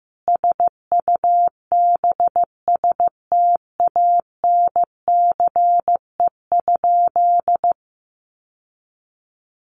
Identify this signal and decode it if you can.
Morse code, heard as a keyed tone, about 15 wpm, 700 Hz. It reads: SUBSTANCE?
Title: Sentences from Top 1500 Words 15wpm